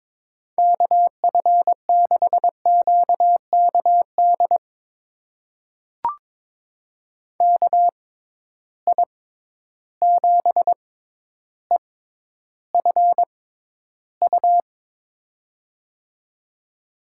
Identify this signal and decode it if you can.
Morse code, heard as a keyed tone, about 21 wpm, spaced as for 11 wpm, 700 Hz. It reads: KF6QKD KI7EFU
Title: USA Call Signs - 6 Characters - 6x Character Spacing - 22wpm